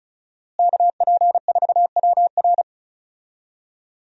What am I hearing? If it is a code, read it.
Morse code, heard as a keyed tone, about 36 wpm, 700 Hz. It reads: KP4WR
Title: USA Call Signs - 5 Characters - Repeated 2x - 35wpm